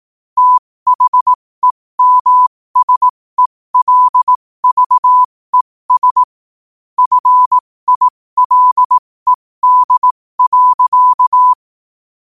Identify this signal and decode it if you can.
Morse code, heard as a keyed tone, about 18 wpm, spaced as for 15 wpm, 1000 Hz. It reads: THEMSELVES FILED.
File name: qotdfiles / 15wpm